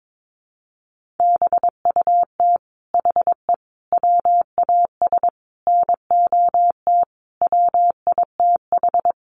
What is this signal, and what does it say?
Morse code, heard as a keyed tone, about 22 wpm, 700 Hz. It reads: BUT HE WAS NOT WITH
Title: Sentences from Top 1500 Words 22wpm